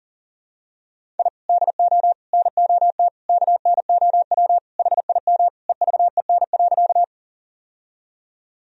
Morse code, heard as a keyed tone, about 37 wpm, 700 Hz. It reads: I DO NOT KNOW HIM EVEN.
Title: Sentences from Top 1000 Words - Review Part 1 - 40wpm